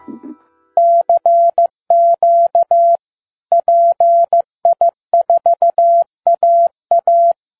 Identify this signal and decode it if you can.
Morse code, heard as a keyed tone, about 15 wpm, 680 Hz. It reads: CQ PI4AA
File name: pi4aa / PI4AA Uitzending